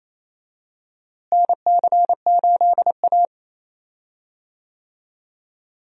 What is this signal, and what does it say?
Morse code, heard as a keyed tone, about 28 wpm, 700 Hz. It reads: NC8A
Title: USA Call Signs - 4 Characters - Repeated 3x - 28wpm